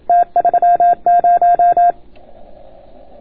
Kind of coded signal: Morse code